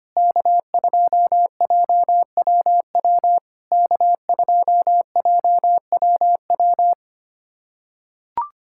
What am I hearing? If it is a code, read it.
Morse code, heard as a keyed tone, about 25 wpm, 690 Hz. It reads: K2JWW K2JWW
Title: SOTA QSO Practice 25wpm